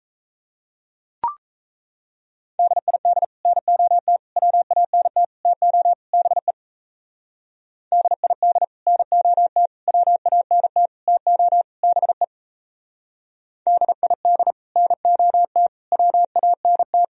Morse code, heard as a keyed tone, about 36 wpm, 700 Hz. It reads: DID NOT WANT TO BE DID NOT WANT TO BE DID NOT WANT